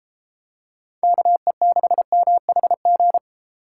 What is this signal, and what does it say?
Morse code, heard as a keyed tone, about 32 wpm, 700 Hz. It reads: KE6MHG